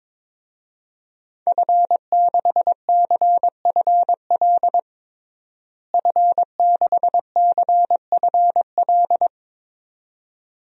Morse code, heard as a keyed tone, about 22 wpm, 700 Hz. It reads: F6CFL F6CFL
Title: DXCC-100 Call Signs - 5 Characters - Repeated 2x - 22wpm